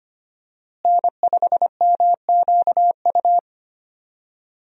Morse code, heard as a keyed tone, about 25 wpm, 700 Hz. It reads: N5MQU
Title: USA Call Signs - 5 Characters - Repeated 2x - 25wpm